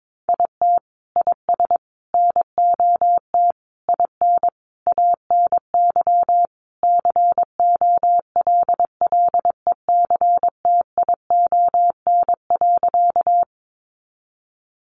Morse code, heard as a keyed tone, about 22 wpm, 700 Hz. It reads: IT IS NOT IN ANY COLLECTION.
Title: Sentences from Top 1200 Words 22wpm